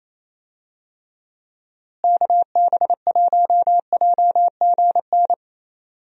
Morse code, heard as a keyed tone, about 28 wpm, 700 Hz. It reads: KB1JGN
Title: USA Call Signs - 6 Characters - Repeated 3x - 28wpm